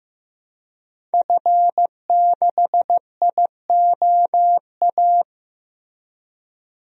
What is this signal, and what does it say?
Morse code, heard as a keyed tone, about 14 wpm, 700 Hz. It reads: F6IOA
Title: DXCC-100 Call Signs - 5 Characters - Repeated 3x - 15wpm